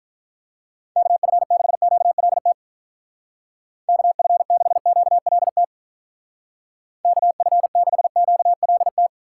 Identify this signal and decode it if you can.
Morse code, heard as a keyed tone, about 44 wpm, 700 Hz. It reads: KF6QLT KF6QLT KF6QLT
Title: Speed Racing - USA Call Signs 35wpm